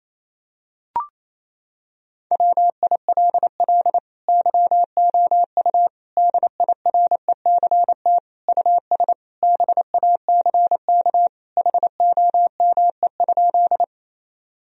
Morse code, heard as a keyed tone, about 27 wpm, 700 Hz. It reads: WILL YOU DIRECT US BACK HOME?
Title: Sentences from Top 1000 Words - Review Part 2 - 28wpm